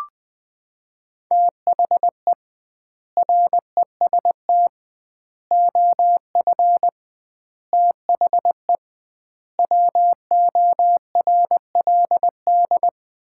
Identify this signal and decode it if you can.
Morse code, heard as a keyed tone, about 20 wpm, 700 Hz. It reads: THE REST OF THE WORLD
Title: 2x Word-Spacing - Sets of 5 Words - 20wpm